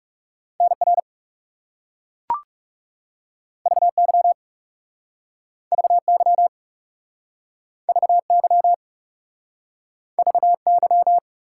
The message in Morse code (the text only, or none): NR VY VY VY VY